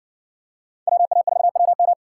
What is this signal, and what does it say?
Morse code, heard as a keyed tone, about 56 wpm, 700 Hz. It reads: WA4WM